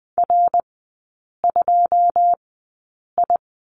Morse code, heard as a keyed tone, about 20 wpm, spaced as for 8 wpm, 700 Hz. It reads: R2I
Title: 107 Lesson 18e - Letter m - Three Characters 20wpm